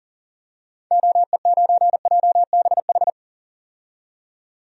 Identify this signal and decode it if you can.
Morse code, heard as a keyed tone, about 40 wpm, 700 Hz. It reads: OE9JBH
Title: DXCC-100 Call Signs - 6 Characters - 40wpm